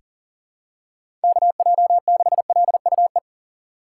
Morse code, heard as a keyed tone, about 40 wpm, 700 Hz. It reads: KJ6LUE
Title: USA Call Signs - 6 Characters - 6x Character Spacing - 40wpm